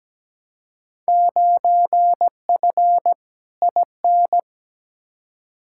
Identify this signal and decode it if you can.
Morse code, heard as a keyed tone, about 17 wpm, 700 Hz. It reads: 9F IN
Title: Field Day Exchanges Encore - 17wpm